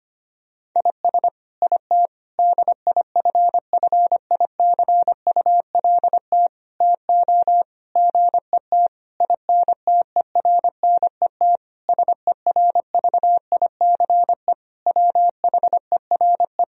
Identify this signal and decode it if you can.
Morse code, heard as a keyed tone, about 25 wpm, 700 Hz. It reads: IS IT DIFFICULT TO GET INTERNET SERVICE WHERE
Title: Sentences from Top 1000 Words - Review Part 4 - 25wpm